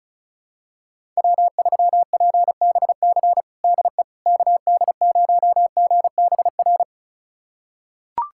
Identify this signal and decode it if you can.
Morse code, heard as a keyed tone, about 34 wpm, 700 Hz. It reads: W3PBC DE KD0GBR
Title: USA Callsign DE Callsign 35wpm